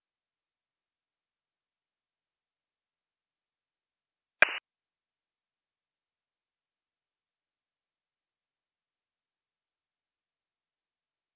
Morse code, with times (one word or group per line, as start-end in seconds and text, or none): none